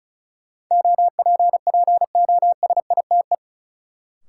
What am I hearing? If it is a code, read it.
Morse code, heard as a keyed tone, about 36 wpm, 700 Hz. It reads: OPPOSITE
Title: Sets of 5 Words Encore - New Word Drill - 35wpm